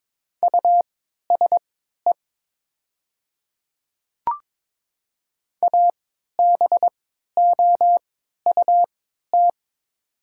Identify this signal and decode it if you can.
Morse code, heard as a keyed tone, about 22 wpm, spaced as for 12 wpm, 700 Hz. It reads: USE ABOUT
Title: 3x Character Spacing - Top 100 Words - 22wpm